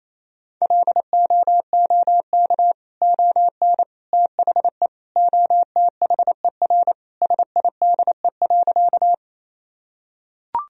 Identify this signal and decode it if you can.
Morse code, heard as a keyed tone, about 28 wpm, 700 Hz. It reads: LOOK ON THE OTHER SIDE.
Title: Sentences from Top 1000 Words - Review Part 1 - 28wpm